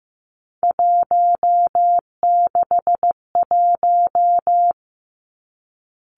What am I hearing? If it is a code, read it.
Morse code, heard as a keyed tone, about 15 wpm, 700 Hz. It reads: 161
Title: Three Challenging Letter-Number Combinations 15wpm